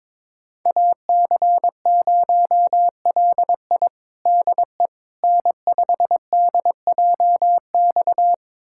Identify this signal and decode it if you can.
Morse code, heard as a keyed tone, about 21 wpm, 700 Hz. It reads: AC0LI DE N5DJX
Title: USA Callsign DE Callsign 22wpm